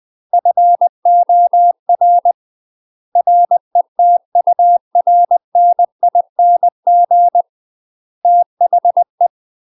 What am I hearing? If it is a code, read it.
Morse code, heard as a keyed tone, about 20 wpm, 700 Hz. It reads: FOR RETURNING THE
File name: UK MASTER QSOs TRAINING / 20WPM / UK MASTER QSOs 0034